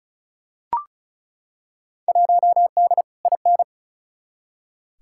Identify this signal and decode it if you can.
Morse code, heard as a keyed tone, about 34 wpm, 700 Hz. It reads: 1D IN
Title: Field Day Exchanges - 35wpm